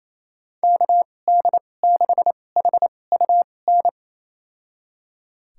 Morse code, heard as a keyed tone, about 28 wpm, spaced as for 20 wpm, 700 Hz. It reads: KD6HUN
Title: USA Call Signs - 6 Characters - 2x Character Spacing - 28wpm